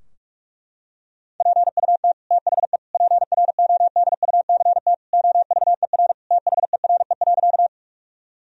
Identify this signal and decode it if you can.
Morse code, heard as a keyed tone, about 46 wpm, 700 Hz. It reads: PUT THE PRODUCT OVER THERE.
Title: Sentences from Top 1000 Words - Review Part 2 - 45wpm